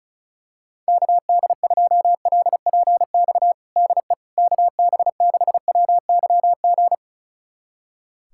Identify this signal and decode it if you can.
Morse code, heard as a keyed tone, about 36 wpm, 700 Hz. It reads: KD2LPX DE KB6WYG
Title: USA Callsign DE Callsign - 6 Characters - 35wpm